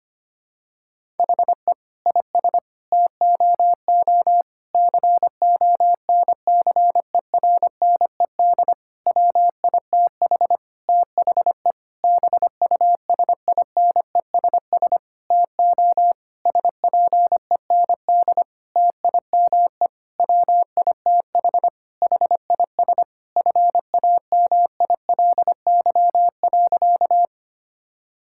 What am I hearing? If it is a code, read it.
Morse code, heard as a keyed tone, about 25 wpm, 700 Hz. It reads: HE IS TOO CONCERNED WITH THE BUSINESS TO SPEND TIME WITH HIS FAMILY.
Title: Sentences from Top 1000 Words - Review Part 6 - 25wpm